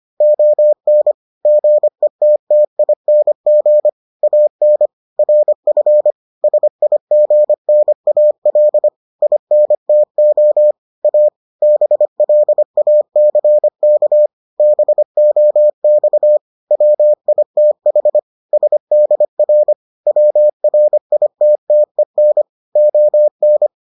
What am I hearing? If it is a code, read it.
Morse code, heard as a keyed tone, about 25 wpm, 600 Hz. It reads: ON GETTING AN RF SIGNAL INTO A BLACK BOX WITH SDR WRITTEN ON